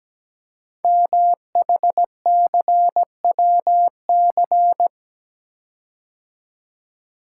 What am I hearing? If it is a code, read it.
Morse code, heard as a keyed tone, about 17 wpm, 700 Hz. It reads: MHCWC
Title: Code Groups - 5 Letter:Numbers - 17wpm